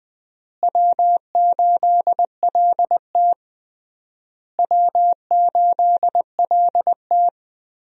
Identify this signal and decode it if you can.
Morse code, heard as a keyed tone, about 20 wpm, 700 Hz. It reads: W8LT W8LT
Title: USA Call Signs - 4 Characters - Repeated 3x - 20wpm